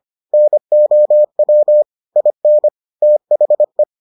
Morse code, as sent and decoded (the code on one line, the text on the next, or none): -. --- .-- / .. -. / - .... .
NOW IN THE